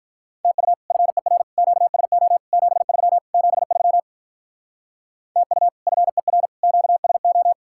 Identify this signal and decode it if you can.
Morse code, heard as a keyed tone, about 44 wpm, 700 Hz. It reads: TU FER QSO 73 73 TU FER QSO